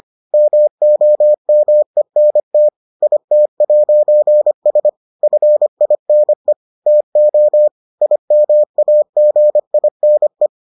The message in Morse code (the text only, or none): MOMENT IT'S FINE TO IMAGINE